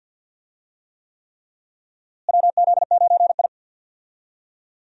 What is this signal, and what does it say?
Morse code, heard as a keyed tone, about 47 wpm, 700 Hz. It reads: WZ9I
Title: USA Call Signs - 4 Characters - Repeated 3x - 50wpm